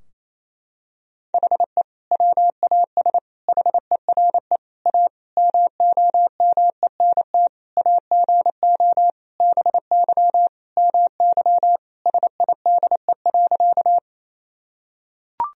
Morse code, heard as a keyed tone, about 28 wpm, 700 Hz. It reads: HE WAS HERE A MOMENT AGO BY MY SIDE.
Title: Sentences from Top 1000 Words - Review Part 3 - 28wpm